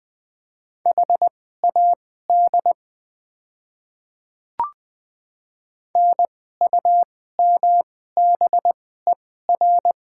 Morse code, heard as a keyed tone, about 20 wpm, spaced as for 15 wpm, 700 Hz. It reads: HAD NUMBER
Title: 2x Character Spacing - Top 100 Words - 20wpm